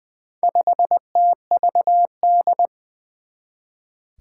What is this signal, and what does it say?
Morse code, heard as a keyed tone, about 20 wpm, 700 Hz. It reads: 5TVD
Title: Four Challenging Letter-Number Combinations 20wpm